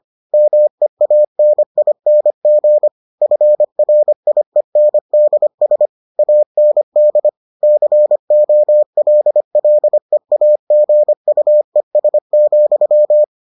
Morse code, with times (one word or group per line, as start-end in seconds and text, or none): 0.3-2.9 MEANING
3.2-5.9 FRIENDS
6.2-7.3 AND
7.6-13.2 COLLEAGUES,